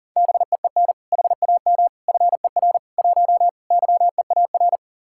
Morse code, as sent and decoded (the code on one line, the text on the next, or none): -... . -. / .... .- -- / ..-. .-. / .---- / -.-- . .- .-.
BEEN HAM FER 1 YEAR